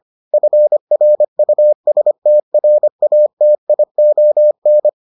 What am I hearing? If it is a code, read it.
Morse code, heard as a keyed tone, about 25 wpm, 600 Hz. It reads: FRUSTRATION